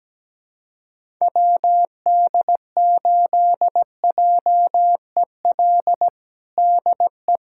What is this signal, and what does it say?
Morse code, heard as a keyed tone, about 17 wpm, 700 Hz. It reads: WD8JEL DE
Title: USA Callsign DE Callsign - 6 Characters - 17wpm